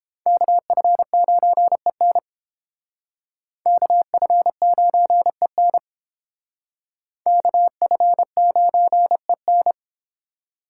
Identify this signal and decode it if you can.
Morse code, heard as a keyed tone, about 30 wpm, 700 Hz. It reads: KF9EN KF9EN KF9EN